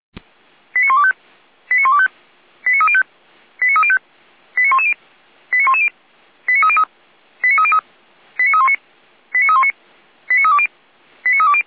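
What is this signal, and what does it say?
Five-tone selective call (ZVEI), heard as quick runs of five tones, tones of about 70 ms. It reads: 89125 89125 89305 89305 89110 89110 89303 89303 89319 89319 89320 89320
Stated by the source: ZVEI five-tone selective call received on an SDR.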